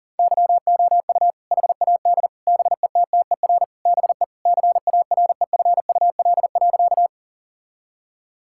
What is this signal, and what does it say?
Morse code, heard as a keyed tone, about 37 wpm, 700 Hz. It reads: YOU HAD BETTER BE CAREFUL.
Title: Sentences from Top 1600 Words 40wpm